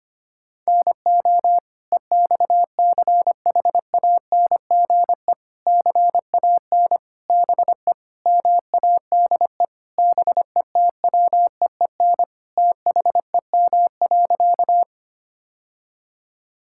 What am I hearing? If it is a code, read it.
Morse code, heard as a keyed tone, about 25 wpm, 700 Hz. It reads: NO EXCHANGE CAN BE MADE BETWEEN THEM.
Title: Sentences from Top 1000 Words - Review Part 6 - 25wpm